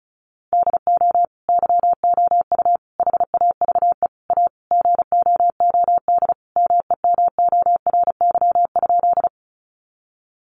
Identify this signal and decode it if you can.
Morse code, heard as a keyed tone, about 34 wpm, 700 Hz. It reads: DO YOU HAVE A GOOD MEMORY?
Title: Sentences from Top 1300 Words 35wpm